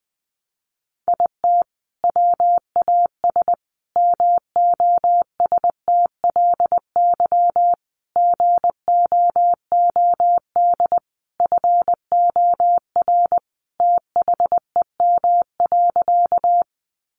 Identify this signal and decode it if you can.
Morse code, heard as a keyed tone, about 20 wpm, 700 Hz. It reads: IT WAS MOSTLY GOOD FOR THEM.
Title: Sentences from Top 1300 Words 20wpm